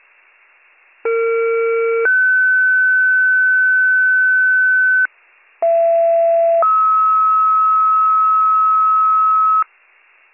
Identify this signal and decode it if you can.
Two-tone page, heard as a one-second tone, then a three-second tone, about 460 then 1500 Hz, about 670 then 1290 Hz. It is heard from 1.0 to 9.6 s.